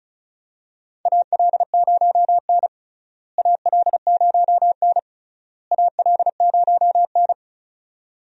Morse code, heard as a keyed tone, about 34 wpm, 700 Hz. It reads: AL0N AL0N AL0N
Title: USA Call Signs - 4 Characters - Repeated 3x - 35wpm